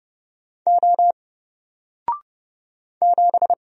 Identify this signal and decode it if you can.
Morse code, heard as a keyed tone, about 30 wpm, 700 Hz. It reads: O 7
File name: Single-Letter-Number-30wpm